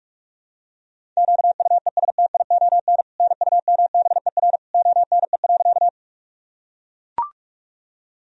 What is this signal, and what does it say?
Morse code, heard as a keyed tone, about 43 wpm, 700 Hz. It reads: QUESTION NUMBER ONE.
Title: Sentences from Top 1000 Words - Review Part 1 - 45wpm